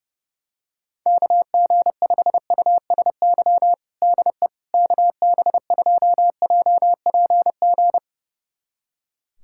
Morse code, heard as a keyed tone, about 30 wpm, 700 Hz. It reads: KG5USY DE KB2JPG
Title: USA Callsign DE Callsign - 6 Characters - 30wpm